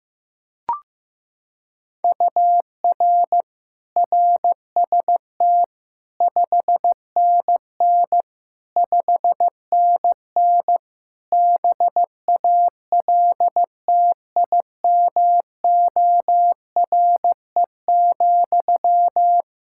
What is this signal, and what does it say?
Morse code, heard as a keyed tone, about 15 wpm, 700 Hz. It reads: UR RST 5NN 5NN BALTIMORE,
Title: Practice QSO - Exchange 1 - Beginning - 15wpm